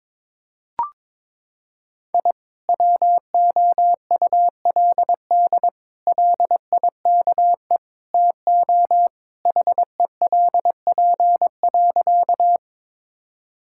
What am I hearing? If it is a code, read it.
Morse code, heard as a keyed tone, about 21 wpm, 700 Hz. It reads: I WOULD LIKE TO HELP.
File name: practice / 22wpm / Sentences with 4-Word Phrases - 22wpm